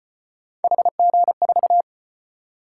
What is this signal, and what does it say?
Morse code, heard as a keyed tone, about 32 wpm, 700 Hz. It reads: HG4